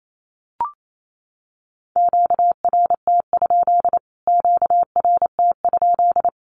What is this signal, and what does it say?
Morse code, heard as a keyed tone, about 28 wpm, 700 Hz. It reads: QRT? QRT?
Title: Q-Codes Repeated 3x 28wpm